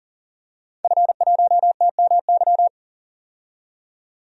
Morse code, heard as a keyed tone, about 37 wpm, 700 Hz. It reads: F1TMY
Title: DXCC-100 Call Signs - 5 Characters - 40wpm